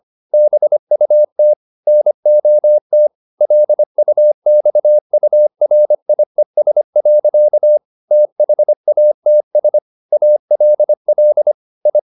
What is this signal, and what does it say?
Morse code, heard as a keyed tone, about 25 wpm, 600 Hz. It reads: BUT NOT LUXURIES. THATS ALL I